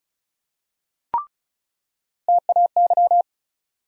Morse code, heard as a keyed tone, about 34 wpm, 700 Hz. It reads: TAY
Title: USA Call Sign Postfixes - 3 Letters 35wpm